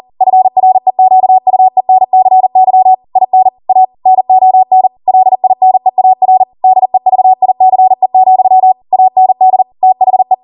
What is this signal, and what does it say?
Morse code, heard as a keyed tone, about 40 wpm, 750 Hz. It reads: FREQUENCY IN A NON LINEAR DEVICE, AND THE
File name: Morse / 40WPM